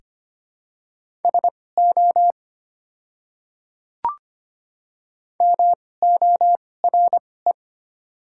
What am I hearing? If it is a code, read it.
Morse code, heard as a keyed tone, about 25 wpm, spaced as for 18 wpm, 700 Hz. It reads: SO MORE
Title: 2x Character Spacing - Top 100 Words - 25wpm